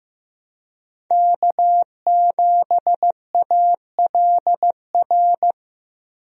Morse code, heard as a keyed tone, about 15 wpm, 700 Hz. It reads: K7ALR
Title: USA Call Signs - 5 Characters - Repeated 2x - 15wpm